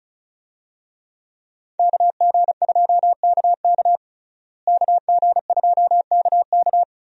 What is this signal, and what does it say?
Morse code, heard as a keyed tone, about 36 wpm, 700 Hz. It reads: KG2KK KG2KK
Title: USA Call Signs - 5 Characters - Repeated 3x - 35wpm